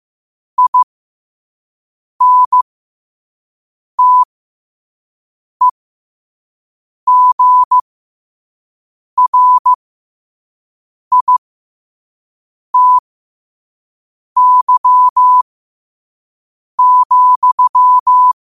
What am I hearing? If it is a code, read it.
Morse code, heard as a keyed tone, about 15 wpm, spaced as for 5 wpm, 1000 Hz. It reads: INTEGRITY,
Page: New quotes every day in morse code at 5 Words per minute.